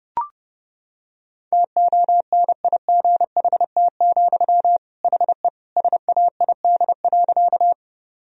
Morse code, heard as a keyed tone, about 32 wpm, 700 Hz. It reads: TONIGHT, HE SAID.
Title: Sentences from Top 1000 Words - Review Part 3 - 30wpm